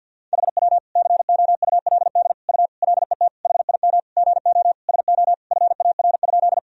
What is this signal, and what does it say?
Morse code, heard as a keyed tone, about 52 wpm, 700 Hz. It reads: HW COULD U LET HIM GO SO FAR?
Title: Sentences from Top 300 Words and Abbreviations 50wpm